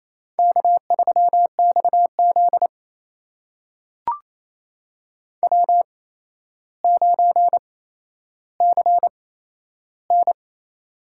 Morse code, heard as a keyed tone, about 28 wpm, spaced as for 12 wpm, 700 Hz. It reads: K3XZ W9CN
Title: USA Call Signs - 4 Characters - 8x Character Spacing - 28wpm